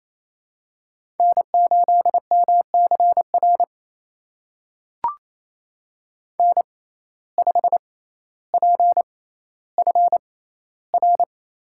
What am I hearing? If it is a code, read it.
Morse code, heard as a keyed tone, about 28 wpm, spaced as for 14 wpm, 700 Hz. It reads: N8MCR N5PFR
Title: USA Call Signs - 5 Characters - 6x Character Spacing - 28wpm